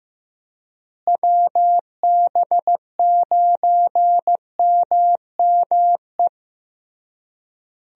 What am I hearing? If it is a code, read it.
Morse code, heard as a keyed tone, about 15 wpm, 700 Hz. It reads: WB9MME
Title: USA Call Signs - 6 Characters - Repeated 3x - 15wpm